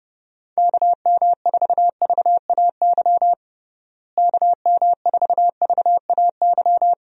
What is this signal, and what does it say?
Morse code, heard as a keyed tone, about 30 wpm, 700 Hz. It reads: KM4VAY KM4VAY
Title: USA Call Signs - 6 Characters - Repeated 2x - 30wpm